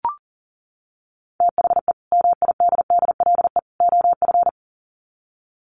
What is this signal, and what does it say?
Morse code, heard as a keyed tone, about 40 wpm, 700 Hz. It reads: THE MIDDLE OF
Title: Sets of 3 Words 40wpm